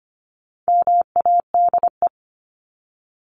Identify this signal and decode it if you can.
Morse code, heard as a keyed tone, about 25 wpm, 700 Hz. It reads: MADE
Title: Top 300 Words Repeated 25wpm